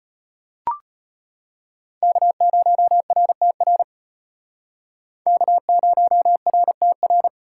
Morse code, heard as a keyed tone, about 32 wpm, 700 Hz. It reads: K0RTR K0RTR